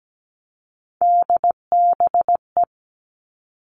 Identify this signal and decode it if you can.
Morse code, heard as a keyed tone, about 17 wpm, 700 Hz. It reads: DBE
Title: All Cut Numbers - 3-Digit 17wpm